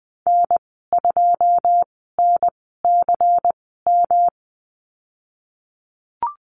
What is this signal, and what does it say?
Morse code, heard as a keyed tone, about 20 wpm, spaced as for 15 wpm, 700 Hz. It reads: N2NCM
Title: 122 Lesson 20g - Number 3 - Call Signs 20wpm